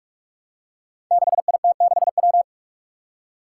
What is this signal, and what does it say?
Morse code, heard as a keyed tone, about 43 wpm, 700 Hz. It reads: 6IT6W
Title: Five Challenging Letter-Number Combinations 45wpm